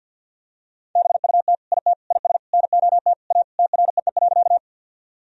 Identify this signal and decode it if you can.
Morse code, heard as a keyed tone, about 50 wpm, 700 Hz. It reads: BUT IT IS NOT A TREE.
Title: Sentences from Top 1000 Words - Review Part 1 - 50wpm